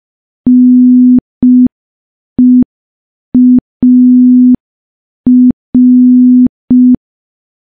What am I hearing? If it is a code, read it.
Morse code, heard as a keyed tone, about 5 wpm, 250 Hz. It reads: NEAR
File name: morse / morse-training-05WPM